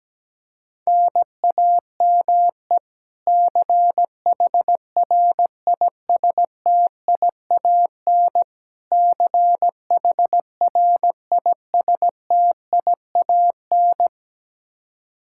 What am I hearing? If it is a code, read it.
Morse code, heard as a keyed tone, about 17 wpm, 700 Hz. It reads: NAME CHRISTIAN CHRISTIAN
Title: Practice QSO - Exchange 1 - Beginning - 17wpm